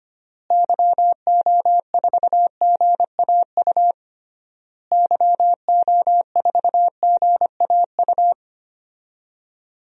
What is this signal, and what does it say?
Morse code, heard as a keyed tone, about 25 wpm, 690 Hz. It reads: YO4GAU YO4GAU
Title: DXCC-100 Call Signs - 6 Characters - Repeated 3x - 25wpm